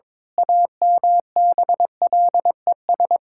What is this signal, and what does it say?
Morse code, heard as a keyed tone, about 22 wpm, 700 Hz. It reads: AMBLES
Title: Pangram Challenge v2 - 22wpm